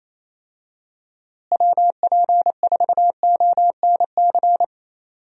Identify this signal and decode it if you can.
Morse code, heard as a keyed tone, about 28 wpm, 700 Hz. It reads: WP4ONC
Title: USA Call Signs - 6 Characters - Repeated 3x - 28wpm